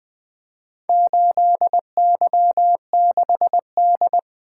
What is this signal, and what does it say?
Morse code, heard as a keyed tone, about 20 wpm, 700 Hz. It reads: 8Y6D
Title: Four Letter-Number Combinations - 20wpm